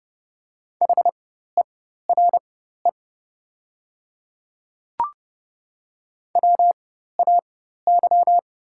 Morse code, heard as a keyed tone, about 30 wpm, spaced as for 14 wpm, 700 Hz. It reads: HERE WAY
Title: 4x Character Spacing - Top 200 Words - 30wpm